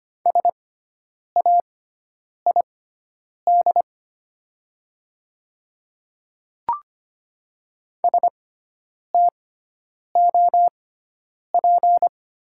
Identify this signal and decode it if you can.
Morse code, heard as a keyed tone, about 25 wpm, spaced as for 9 wpm, 700 Hz. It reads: SAID STOP
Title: 6x Character Spacing - Top 300 Words - 25wpm